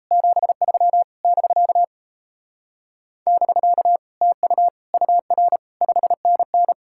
Morse code, heard as a keyed tone, about 38 wpm, 700 Hz. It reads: 73 <BK> <BK> TU UR 5NN